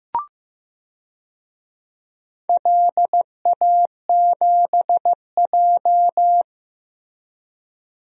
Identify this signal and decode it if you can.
Morse code, heard as a keyed tone, about 16 wpm, 700 Hz. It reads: LA7J
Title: DXCC-100 Call Signs - 4 Characters - Repeated 3x - 15wpm